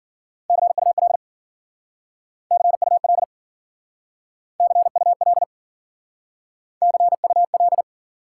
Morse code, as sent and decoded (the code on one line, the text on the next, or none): -.-. ..- .-.. / -.-. ..- .-.. / -.-. ..- .-.. / -.-. ..- .-..
CUL CUL CUL CUL